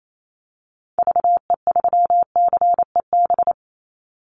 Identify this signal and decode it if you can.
Morse code, heard as a keyed tone, about 28 wpm, 700 Hz. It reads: VE3CEB
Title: Call Signs Repeated 2x - 28wpm